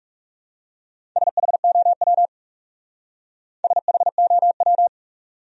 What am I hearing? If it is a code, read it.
Morse code, heard as a keyed tone, about 37 wpm, 700 Hz. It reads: SHOW SHOW
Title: Speed-Racing Top 200 Words 30wpm